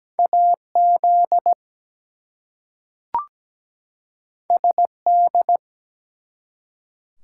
Morse code, heard as a keyed tone, about 17 wpm, 700 Hz. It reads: AZ SD
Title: US States and Canadian Provinces 17wpm